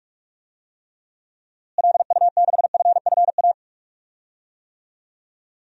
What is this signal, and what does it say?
Morse code, heard as a keyed tone, about 46 wpm, 700 Hz. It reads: LU6FFA
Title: DXCC-100 Call Signs - 6 Characters - Repeated 3x - 45wpm